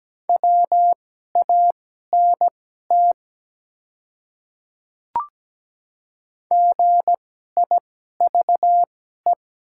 Morse code, heard as a keyed tone, about 17 wpm, spaced as for 12 wpm, 700 Hz. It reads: WANT GIVE
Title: 2x Character Spacing - Top 200 Words - 17wpm